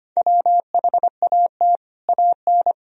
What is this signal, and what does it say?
Morse code, heard as a keyed tone, about 25 wpm, 700 Hz. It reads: WHAT AN
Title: Sentences from Top 1000 Words - Review Part 1 - 25wpm